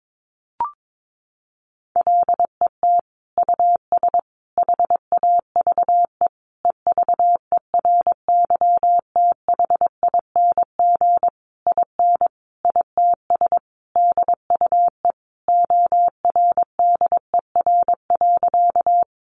Morse code, heard as a keyed tone, about 22 wpm, 700 Hz. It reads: LET US HAVE EVERYTHING IN ITS DUE ORDER.
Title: Sentences from Top 900 Words 22wpm